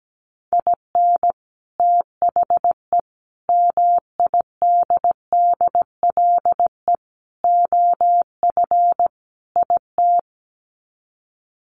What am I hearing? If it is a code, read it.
Morse code, heard as a keyed tone, about 17 wpm, 700 Hz. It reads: IN THE MIDDLE OF IT
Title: Sets of 5 Words Encore 17wpm